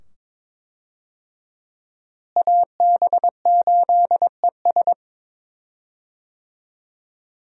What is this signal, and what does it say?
Morse code, heard as a keyed tone, about 21 wpm, 700 Hz. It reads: AB8ES
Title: USA Call Signs - 5 Characters - Repeated 2x - 22wpm